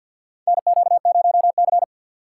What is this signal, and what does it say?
Morse code, heard as a keyed tone, about 52 wpm, 690 Hz. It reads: NQ0C
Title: USA Call Signs - 3x Character Spacing - 50wpm